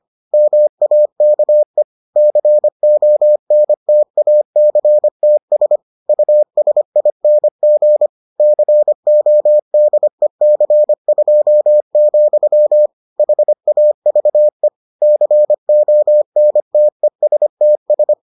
Morse code, heard as a keyed tone, about 25 wpm, 600 Hz. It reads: MAKE CONTACTS USING CODEC2, HAVE CONTESTS